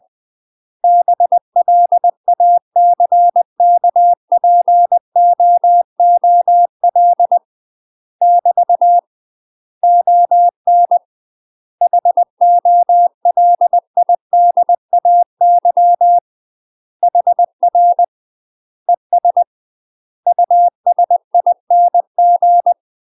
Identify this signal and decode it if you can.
Morse code, heard as a keyed tone, about 20 wpm, 700 Hz. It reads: BLACKPOOL <BT> ON HOLIDAY HR ES USING
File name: UK MASTER QSOs TRAINING / 20WPM / UK MASTER QSOs 0040